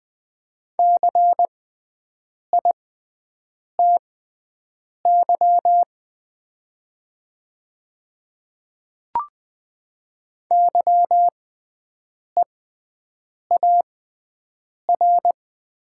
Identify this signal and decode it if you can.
Morse code, heard as a keyed tone, about 20 wpm, spaced as for 7 wpm, 700 Hz. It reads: CITY YEAR
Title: 6x Character Spacing - Top 500 Words - 20wpm